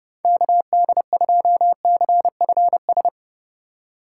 Morse code, heard as a keyed tone, about 30 wpm, 700 Hz. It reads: KD2CFS